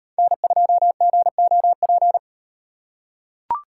Morse code, heard as a keyed tone, about 38 wpm, 700 Hz. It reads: N2GOP